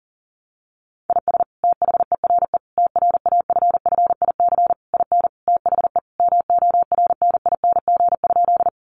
Morse code, heard as a keyed tone, about 37 wpm, 700 Hz. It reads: IS THERE TRAFFIC IN THE MORNING?
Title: Sentences from Top 1300 Words 40wpm